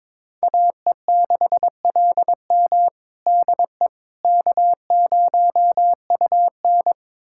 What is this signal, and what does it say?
Morse code, heard as a keyed tone, about 21 wpm, 700 Hz. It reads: AE6LM DE K0UN
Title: USA Callsign DE Callsign 22wpm